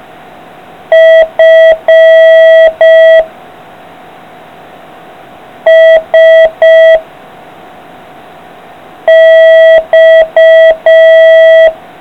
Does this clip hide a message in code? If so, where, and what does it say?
Yes: Morse code from 0.9 to 11.7 s, heard as a keyed tone, about 5 wpm, 660 Hz. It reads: F S X